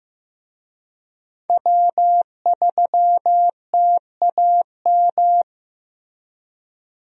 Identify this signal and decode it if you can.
Morse code, heard as a keyed tone, about 15 wpm, 700 Hz. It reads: W3TAM
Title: USA Call Signs - 5 Characters - Repeated 3x - 15wpm